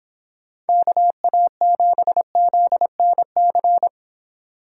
Morse code, heard as a keyed tone, about 29 wpm, 700 Hz. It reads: KA7ZNC